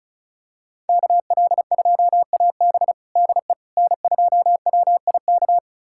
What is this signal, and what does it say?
Morse code, heard as a keyed tone, about 36 wpm, 700 Hz. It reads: KL2AB DE N2WIK
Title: USA Callsign DE Callsign - 5 Characters - 35wpm